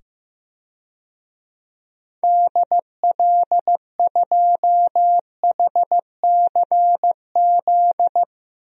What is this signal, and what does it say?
Morse code, heard as a keyed tone, about 15 wpm, 700 Hz. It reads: DL2HCZ
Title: DXCC-100 Call Signs - 6 Characters - 15wpm